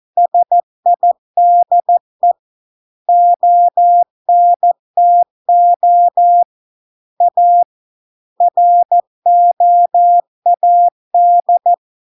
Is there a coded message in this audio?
Yes: Morse code from 0.2 to 11.7 s, heard as a keyed tone, about 14 wpm, 700 Hz. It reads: SIDE ONTO A ROAD